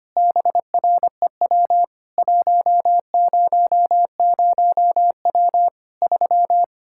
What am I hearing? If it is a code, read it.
Morse code, heard as a keyed tone, about 25 wpm, 700 Hz. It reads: BREW 100W 3
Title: Practice QSO - Exchange 2 - Info - 25wpm